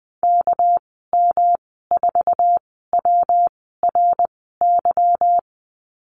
Morse code, heard as a keyed tone, about 20 wpm, spaced as for 15 wpm, 700 Hz. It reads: KM4WRY
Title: 219 Lesson 35g - Number 0 - Call Signs 20wpm